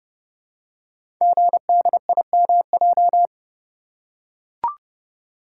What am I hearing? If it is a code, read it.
Morse code, heard as a keyed tone, about 29 wpm, 700 Hz. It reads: GDIMJ
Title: Code Groups - 5 Letters - 30wpm